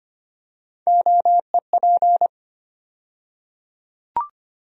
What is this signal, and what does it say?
Morse code, heard as a keyed tone, about 25 wpm, 700 Hz. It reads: OEP
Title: USA Call Sign Postfixes - 3 Letters 25wpm